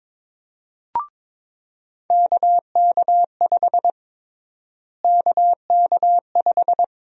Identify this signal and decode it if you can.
Morse code, heard as a keyed tone, about 22 wpm, 700 Hz. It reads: KK5 KK5
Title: USA Call Sign Prefixes - Repeated 2x - 22wpm